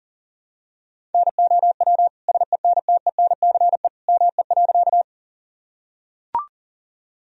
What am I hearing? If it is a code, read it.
Morse code, heard as a keyed tone, about 40 wpm, 700 Hz. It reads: NOW SENTENCE ME.
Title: Sentences from Top 1000 Words - Review Part 1 - 40wpm